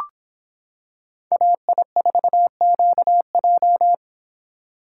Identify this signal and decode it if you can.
Morse code, heard as a keyed tone, about 25 wpm, 700 Hz. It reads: AI4QJ